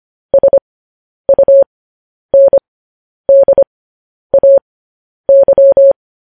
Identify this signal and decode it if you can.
Morse code, heard as a keyed tone, about 25 wpm, spaced as for 10 wpm, 550 Hz. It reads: SUNDAY
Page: News Headlines in Morse Code at 10 WPM